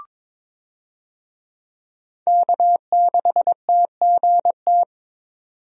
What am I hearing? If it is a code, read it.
Morse code, heard as a keyed tone, about 22 wpm, 700 Hz. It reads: K6TGT
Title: USA Call Signs - 5 Characters - 22wpm